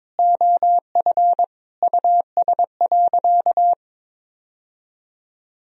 Morse code, heard as a keyed tone, about 22 wpm, 700 Hz. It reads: OF US.
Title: Sentences from Top 1000 Words - Review Part 2 - 22wpm